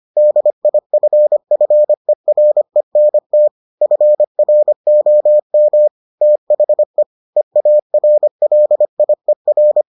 Morse code, heard as a keyed tone, about 25 wpm, 600 Hz. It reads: DIFFERENT FROM THE EARLIER